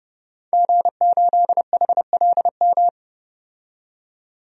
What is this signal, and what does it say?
Morse code, heard as a keyed tone, about 30 wpm, 700 Hz. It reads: G8HLM
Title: DXCC-100 Call Signs - 5 Characters - 30wpm